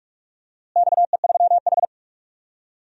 Morse code, heard as a keyed tone, about 43 wpm, 700 Hz. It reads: XE3H
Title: DXCC-100 Call Signs - 4 Characters - Repeated 2x - 45wpm